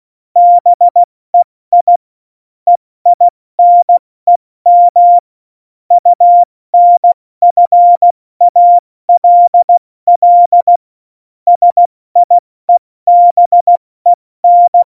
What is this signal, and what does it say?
Morse code, heard as a keyed tone, about 16 wpm, spaced as for 14 wpm, 700 Hz. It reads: BEI EINEM UNFALL SIEBEN
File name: Morsekurs / Witze / Witz 01. Dezember 2020, 70 F80